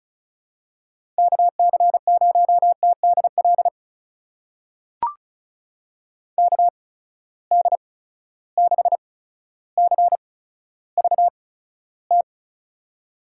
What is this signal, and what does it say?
Morse code, heard as a keyed tone, about 36 wpm, spaced as for 15 wpm, 700 Hz. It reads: KC0TDL KD6CVT
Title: USA Call Signs - 8x Character Spacing - 35wpm